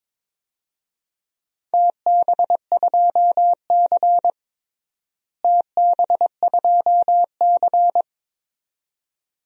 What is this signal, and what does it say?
Morse code, heard as a keyed tone, about 22 wpm, 700 Hz. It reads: TB2C TB2C
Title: DXCC-100 Call Signs - 4 Characters - Repeated 2x - 22wpm